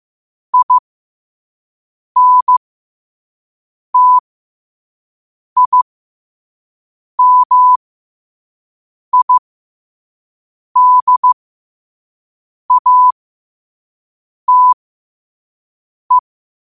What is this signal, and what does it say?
Morse code, heard as a keyed tone, about 15 wpm, spaced as for 5 wpm, 1000 Hz. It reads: INTIMIDATE
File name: qotdfiles / 5wpm